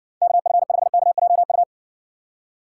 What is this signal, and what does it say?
Morse code, heard as a keyed tone, about 60 wpm, 700 Hz. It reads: KF5KJU